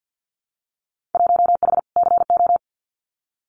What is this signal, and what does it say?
Morse code, heard as a keyed tone, about 50 wpm, 700 Hz. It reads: 1H CO
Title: Winter Field Day Exchanges 50wpm